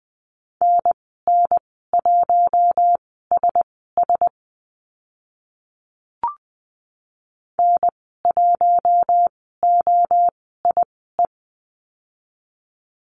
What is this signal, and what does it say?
Morse code, heard as a keyed tone, about 20 wpm, spaced as for 14 wpm, 700 Hz. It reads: NN1SS N1OIE
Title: 042 Lesson 8g - Number 1 - Call Signs 20wpm